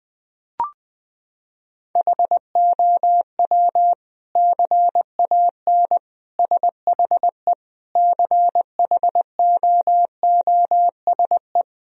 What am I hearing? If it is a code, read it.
Morse code, heard as a keyed tone, about 20 wpm, 700 Hz. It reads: HOW CAN SHE CHOOSE
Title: Sentences from Top 1000 Words - Review Part 4 - 20wpm